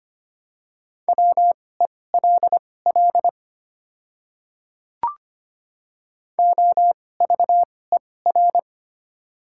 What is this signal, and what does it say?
Morse code, heard as a keyed tone, about 25 wpm, spaced as for 18 wpm, 690 Hz. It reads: WELL OVER
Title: 2x Character Spacing - Top 200 Words - 25wpm